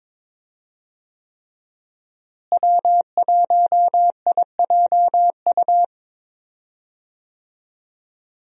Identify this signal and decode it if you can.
Morse code, heard as a keyed tone, about 21 wpm, 700 Hz. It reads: W1IJU
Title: USA Call Signs - 5 Characters - Repeated 3x - 22wpm